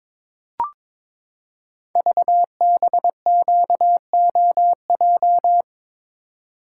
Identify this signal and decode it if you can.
Morse code, heard as a keyed tone, about 22 wpm, 700 Hz. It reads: VBQOJ
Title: Code Groups - 5 Letters - 22wpm